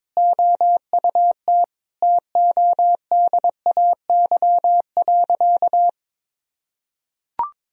Morse code, heard as a keyed tone, about 22 wpm, 700 Hz. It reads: OUT TODAY.
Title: Sentences from Top 1000 Words - Review Part 2 - 22wpm